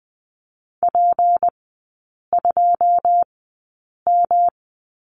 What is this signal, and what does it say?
Morse code, heard as a keyed tone, about 20 wpm, spaced as for 8 wpm, 700 Hz. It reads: P2M
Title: 152 Lesson 25e - Letter p - Three Characters 20wpm